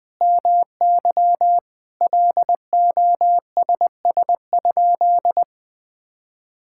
Morse code, heard as a keyed tone, about 20 wpm, 700 Hz. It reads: MY LOSS?
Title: Sentences from Top 1600 Words 20wpm